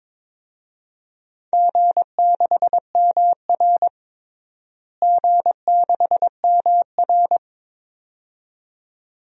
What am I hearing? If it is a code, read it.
Morse code, heard as a keyed tone, about 22 wpm, 700 Hz. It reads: G6MR G6MR
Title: DXCC-100 Call Signs - 4 Characters - Repeated 2x - 22wpm